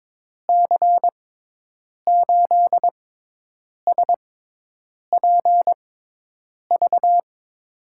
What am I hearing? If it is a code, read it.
Morse code, heard as a keyed tone, about 22 wpm, spaced as for 8 wpm, 700 Hz. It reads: C8SPV
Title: USA Call Signs - 6 Characters - 6x Character Spacing - 22wpm